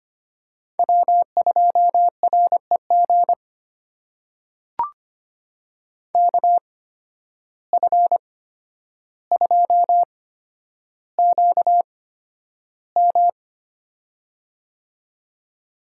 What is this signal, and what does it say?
Morse code, heard as a keyed tone, about 25 wpm, spaced as for 11 wpm, 700 Hz. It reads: W2REG KF2QM
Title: USA Call Signs - 5 Characters - 8x Character Spacing - 25wpm